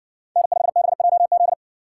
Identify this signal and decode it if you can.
Morse code, heard as a keyed tone, about 53 wpm, 700 Hz. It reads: N5BJZ